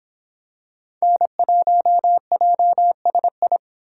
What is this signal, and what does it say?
Morse code, heard as a keyed tone, about 26 wpm, 700 Hz. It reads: N1JSI